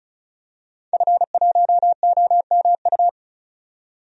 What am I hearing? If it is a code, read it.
Morse code, heard as a keyed tone, about 34 wpm, 700 Hz. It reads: F1OMU
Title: DXCC-100 Call Signs - 5 Characters - 35wpm